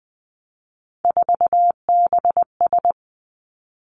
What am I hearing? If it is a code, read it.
Morse code, heard as a keyed tone, about 20 wpm, 700 Hz. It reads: 4BS
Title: Three Challenging Letter-Number Combinations 20wpm